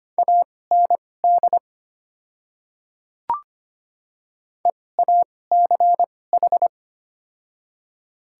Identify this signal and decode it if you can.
Morse code, heard as a keyed tone, about 25 wpm, spaced as for 18 wpm, 700 Hz. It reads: AND EACH
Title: 2x Character Spacing - Top 100 Words - 25wpm